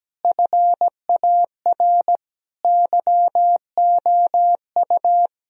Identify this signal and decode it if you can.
Morse code, heard as a keyed tone, about 17 wpm, 700 Hz. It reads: FAR YOU
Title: Sentences from Top 1000 Words - Review Part 1 - 17wpm